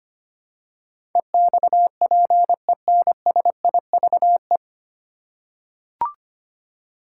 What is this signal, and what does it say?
Morse code, heard as a keyed tone, about 25 wpm, 700 Hz. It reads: EXPENSIVE
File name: Nine Letter Words - 25wpm